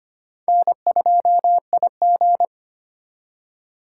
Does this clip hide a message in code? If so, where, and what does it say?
Yes: Morse code from 0.5 to 2.5 s, heard as a keyed tone, about 33 wpm, 700 Hz. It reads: N2IG